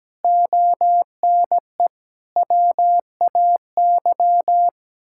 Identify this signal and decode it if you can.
Morse code, heard as a keyed tone, about 17 wpm, 700 Hz. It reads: ONE WAY
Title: Sentences from Top 1000 Words - Review Part 6 - 17wpm